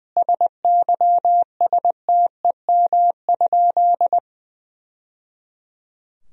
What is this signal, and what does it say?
Morse code, heard as a keyed tone, about 20 wpm, 700 Hz. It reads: SYSTEM?
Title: Sentences from Top 1000 Words - Review Part 3 - 20wpm